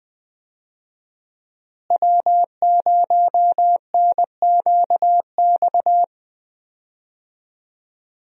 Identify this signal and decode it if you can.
Morse code, heard as a keyed tone, about 20 wpm, 700 Hz. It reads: W0NQX
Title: USA Call Signs - 5 Characters - Repeated 2x - 20wpm